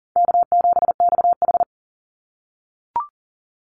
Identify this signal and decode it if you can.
Morse code, heard as a keyed tone, about 40 wpm, 700 Hz. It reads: K7XH
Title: Call Signs - 40wpm